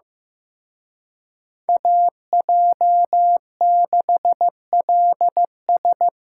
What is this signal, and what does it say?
Morse code, heard as a keyed tone, about 18 wpm, 700 Hz. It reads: AJ6LS
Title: Speed Racing - USA 2x2 Call Signs 15wpm